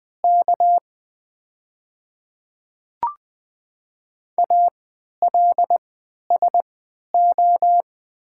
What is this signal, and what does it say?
Morse code, heard as a keyed tone, about 20 wpm, spaced as for 11 wpm, 700 Hz. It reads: K ALSO
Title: 3x Character Spacing - Top 300 Words - 20wpm